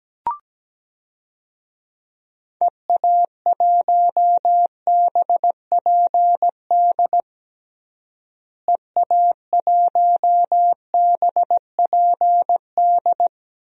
Morse code, heard as a keyed tone, about 17 wpm, 700 Hz. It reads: EA1BPD EA1BPD
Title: DXCC-100 Call Signs - 6 Characters - Repeated 2x - 17wpm